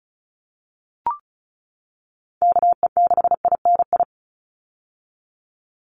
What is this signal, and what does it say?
Morse code, heard as a keyed tone, about 34 wpm, 700 Hz. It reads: KE6INI
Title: USA Call Signs 35wpm